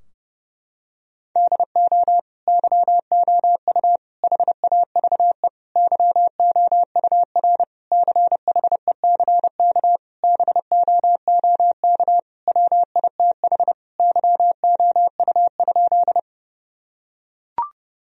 Morse code, heard as a keyed tone, about 30 wpm, 700 Hz. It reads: DO YOU HAVE YOUR CHECK BOOK WITH YOU?
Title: Sentences from Top 1000 Words - Review Part 3 - 30wpm